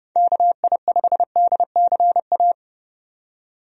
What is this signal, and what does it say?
Morse code, heard as a keyed tone, about 30 wpm, 700 Hz. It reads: KI5DCA
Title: USA Call Signs - 6 Characters - 30wpm